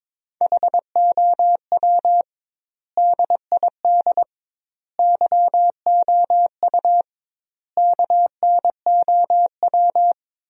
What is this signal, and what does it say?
Morse code, heard as a keyed tone, about 22 wpm, 700 Hz. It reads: HOW DID YOU KNOW
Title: 2x Word-Spacing - Sentences from Top 100 Words - 22wpm